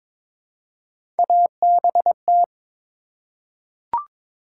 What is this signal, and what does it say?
Morse code, heard as a keyed tone, about 21 wpm, 700 Hz. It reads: ABT
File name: Top 90 QSO Elements - 3x Character Spacing - 22wpm